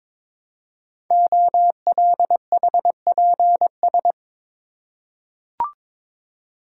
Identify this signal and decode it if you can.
Morse code, heard as a keyed tone, about 22 wpm, 700 Hz. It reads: OLHPS
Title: Code Groups - 5 Letters - 22wpm